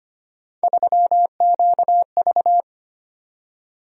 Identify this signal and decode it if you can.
Morse code, heard as a keyed tone, about 25 wpm, 700 Hz. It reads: 3QV
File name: Three Letter-Number Combinations - 25wpm